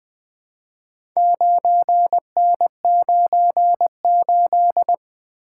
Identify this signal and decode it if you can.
Morse code, heard as a keyed tone, about 20 wpm, 700 Hz. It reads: 9N98
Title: Four Letter-Number Combinations - 20wpm